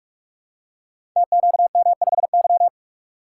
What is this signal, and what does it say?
Morse code, heard as a keyed tone, about 46 wpm, 700 Hz. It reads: TQM5Y
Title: Code Groups - 5 Letter:Numbers - 45wpm